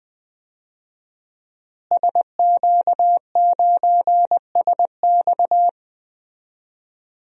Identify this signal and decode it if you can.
Morse code, heard as a keyed tone, about 19 wpm, 700 Hz. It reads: SQ9SX
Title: DXCC-100 Call Signs - 5 Characters - Repeated 3x - 20wpm